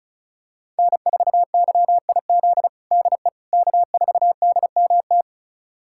Morse code, heard as a keyed tone, about 36 wpm, 700 Hz. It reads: N4YIZ DE K4DMT
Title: USA Callsign DE Callsign - 5 Characters - 35wpm